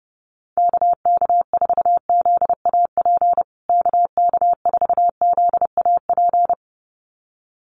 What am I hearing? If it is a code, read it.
Morse code, heard as a keyed tone, about 30 wpm, 700 Hz. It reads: KK4ZAP KK4ZAP
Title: USA Call Signs Repeated 2x 30wpm